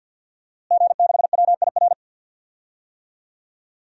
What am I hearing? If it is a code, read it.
Morse code, heard as a keyed tone, about 52 wpm, 700 Hz. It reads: G6WIR
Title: DXCC-100 Call Signs - 5 Characters - Repeated 3x - 50wpm